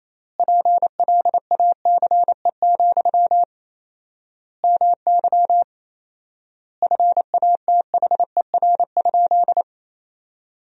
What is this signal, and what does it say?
Morse code, heard as a keyed tone, about 28 wpm, 700 Hz. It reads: PLACE, MY FATHER?
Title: 4x Word-Spacing - Sentences from Top 400 Words - 28wpm